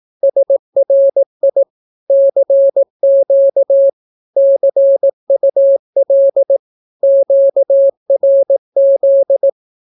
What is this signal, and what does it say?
Morse code, heard as a keyed tone, about 18 wpm, 550 Hz. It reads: SRI CQ CUL QRZ
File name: naukapodstaw / skroty / skroty